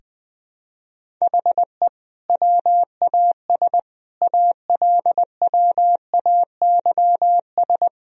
Morse code, heard as a keyed tone, about 20 wpm, 700 Hz. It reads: HE WAS ALWAYS
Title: Sentences from Top 1000 Words - Review Part 2 - 20wpm